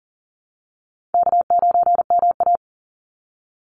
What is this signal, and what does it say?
Morse code, heard as a keyed tone, about 40 wpm, 700 Hz. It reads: K9MA
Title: Easy Call Signs Repeated 2x - 40wpm